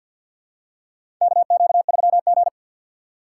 Morse code, heard as a keyed tone, about 49 wpm, 700 Hz. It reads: KQ3G
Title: USA Call Signs - 4 Characters - 6x Character Spacing - 50wpm